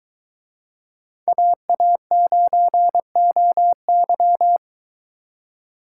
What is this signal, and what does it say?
Morse code, heard as a keyed tone, about 22 wpm, 700 Hz. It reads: AA9OY